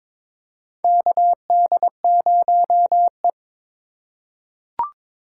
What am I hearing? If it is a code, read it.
Morse code, heard as a keyed tone, about 22 wpm, 700 Hz. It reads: KD0E
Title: USA Call Signs - 4 Characters - 2x Character Spacing - 22wpm